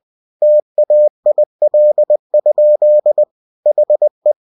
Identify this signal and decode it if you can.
Morse code, heard as a keyed tone, about 20 wpm, 600 Hz. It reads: TAIL? HE
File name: cw-books / winnie-the-pooh / winnie-20wpm-page-0026